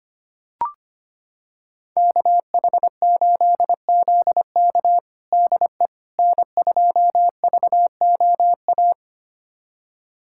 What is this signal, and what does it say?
Morse code, heard as a keyed tone, about 25 wpm, 700 Hz. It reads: KH8ZK DE N2VOA
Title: USA Callsign DE Callsign - 5 Characters - 25wpm